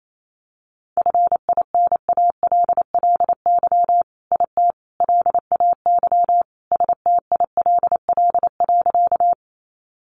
Morse code, heard as a keyed tone, about 28 wpm, 700 Hz. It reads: FINALLY IT LAY STILL.
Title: Sentences from Top 1000 Words 28wpm